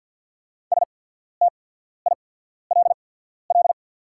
Morse code, heard as a keyed tone, about 48 wpm, spaced as for 14 wpm, 690 Hz. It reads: STILL
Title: 8x Character Spacing - Top 500 Words - 50wpm